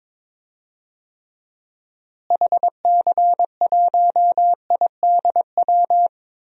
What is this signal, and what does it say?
Morse code, heard as a keyed tone, about 21 wpm, 700 Hz. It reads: HC1IDW
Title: DXCC-100 Call Signs - 6 Characters - Repeated 3x - 22wpm